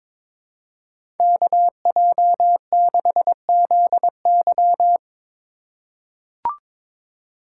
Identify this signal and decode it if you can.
Morse code, heard as a keyed tone, about 23 wpm, 700 Hz. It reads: KJ6ZY
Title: USA Call Signs - 5 Characters - 3x Character Spacing - 22wpm